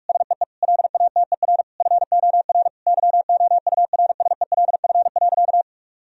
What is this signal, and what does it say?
Morse code, heard as a keyed tone, about 46 wpm, 700 Hz. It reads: SEE LATER FOR YOURSELF.
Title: Sentences from Top 1000 Words - Review Part 4 - 45wpm